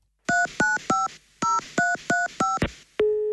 DTMF, heard as telephone touch tones, tones of about 170 ms. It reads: 3950335